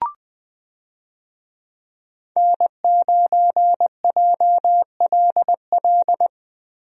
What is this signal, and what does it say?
Morse code, heard as a keyed tone, about 20 wpm, 700 Hz. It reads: N9JLL
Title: USA Call Signs - 5 Characters - Repeated 3x - 20wpm